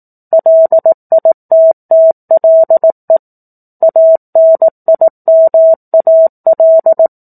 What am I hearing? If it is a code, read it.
Morse code, heard as a keyed tone, about 18 wpm, 650 Hz. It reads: LITTLE ANIMAL
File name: 04-Chapter-04 18 WPM